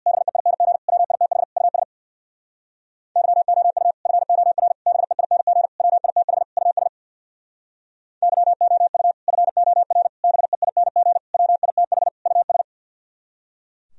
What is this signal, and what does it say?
Morse code, heard as a keyed tone, about 65 wpm, 700 Hz. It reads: BEING WITH US YOU FOR BEING WITH US YOU FOR BEING WITH US